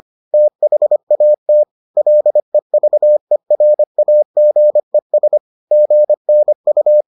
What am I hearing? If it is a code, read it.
Morse code, heard as a keyed tone, about 25 wpm, 600 Hz. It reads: THAT LEVERAGES GNU